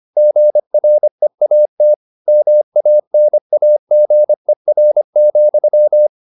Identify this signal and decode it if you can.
Morse code, heard as a keyed tone, about 25 wpm, 600 Hz. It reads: GREAT MANAGER,
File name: the old man and the sea part 1 25wpm